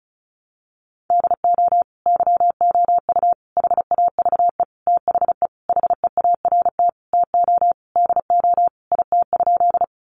Morse code, heard as a keyed tone, about 34 wpm, 700 Hz. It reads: DO YOU HAVE THE HEART TO DO IT?
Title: Sentences from Top 1200 Words 35wpm